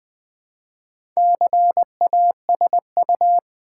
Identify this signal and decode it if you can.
Morse code, heard as a keyed tone, about 20 wpm, 700 Hz. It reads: CASU
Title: Build A Word - 2 Letters At A Time - 8-Letter Words - 20wpm